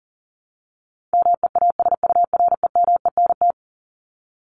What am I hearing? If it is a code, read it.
Morse code, heard as a keyed tone, about 40 wpm, 700 Hz. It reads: MEASUREMENT
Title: Top 1751-2000 Words 40wpm